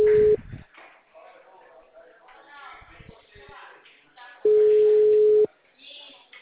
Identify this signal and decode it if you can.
Ringback tone (Europe, 425 Hz), with speech.